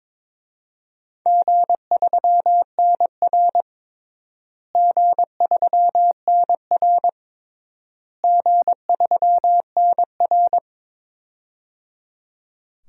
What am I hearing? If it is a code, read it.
Morse code, heard as a keyed tone, about 22 wpm, 700 Hz. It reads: G3NR G3NR G3NR
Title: DXCC-100 Call Signs - 4 Characters - Repeated 3x - 22wpm